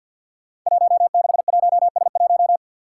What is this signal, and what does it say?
Morse code, heard as a keyed tone, about 51 wpm, 700 Hz. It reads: 161S1
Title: Five Challenging Letter-Number Combinations 50wpm